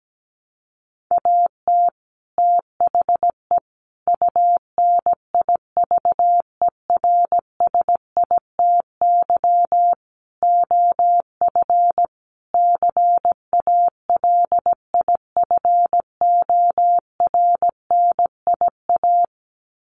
Morse code, heard as a keyed tone, about 17 wpm, 700 Hz. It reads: AT THE UNIVERSITY OF CALIFORNIA
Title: Sets of 5 Words Encore2 17wpm